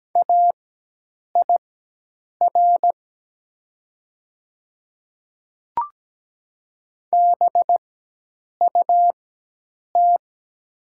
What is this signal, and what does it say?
Morse code, heard as a keyed tone, about 17 wpm, spaced as for 8 wpm, 700 Hz. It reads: AIR BUT
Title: 4x Character Spacing - Top 200 Words - 17wpm